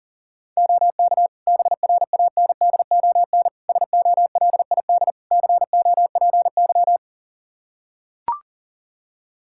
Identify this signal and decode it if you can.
Morse code, heard as a keyed tone, about 40 wpm, 700 Hz. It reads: OK BRANDON SOLID COPY
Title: K1USN SST QSO Practice 40wpm